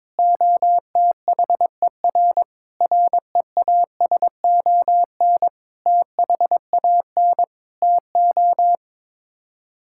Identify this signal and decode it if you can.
Morse code, heard as a keyed tone, about 22 wpm, 700 Hz. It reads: OTHER REASON THAN TO
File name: practice / 22wpm / Sets of 7 Words - 22wpm